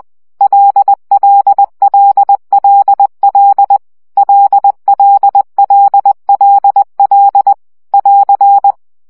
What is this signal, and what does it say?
Morse code, heard as a keyed tone, about 20 wpm, 800 Hz. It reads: LLLLL LLLLL <AR>
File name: CW gyakorlas l 17 baud